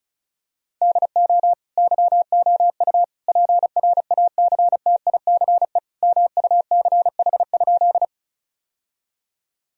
Morse code, heard as a keyed tone, about 34 wpm, 700 Hz. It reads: DO YOU PRACTICE MUCH?
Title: Sentences from Top 1000 Words - Review Part 4 - 35wpm